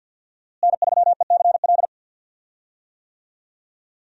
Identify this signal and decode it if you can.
Morse code, heard as a keyed tone, about 47 wpm, 700 Hz. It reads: N3ECL
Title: USA Call Signs - 5 Characters - Repeated 3x - 50wpm